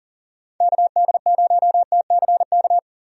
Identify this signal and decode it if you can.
Morse code, heard as a keyed tone, about 37 wpm, 700 Hz. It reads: KD0TCK
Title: USA Call Signs - 6 Characters - 40wpm